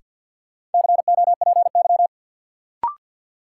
Morse code, heard as a keyed tone, about 47 wpm, 700 Hz. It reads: COPY
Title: Sentences from Top 1400 Words - New Word Drill Alt - 50wpm